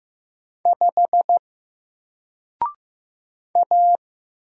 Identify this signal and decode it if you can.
Morse code, heard as a keyed tone, about 15 wpm, 700 Hz. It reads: 5 A
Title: Single Letter-Number 15wpm